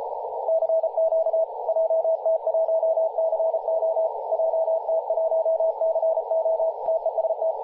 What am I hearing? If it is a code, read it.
Morse code, heard as a keyed tone, about 33 wpm, 650 Hz, with radio noise. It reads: CQ JT1CO JT1CO TEST
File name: posts / 2016-july-17-0522z-jt1co-50-098mhz-cw